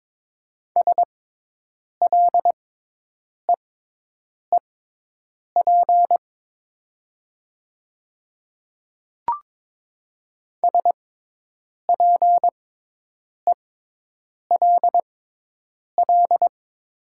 Morse code, heard as a keyed tone, about 22 wpm, spaced as for 8 wpm, 700 Hz. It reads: SLEEP SPELL
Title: 6x Character Spacing - Top 500 Words - 22wpm